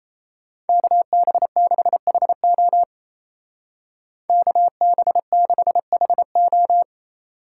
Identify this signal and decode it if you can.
Morse code, heard as a keyed tone, about 33 wpm, 700 Hz. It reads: KB6HO KB6HO
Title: Speed Racing - USA Call Signs 28wpm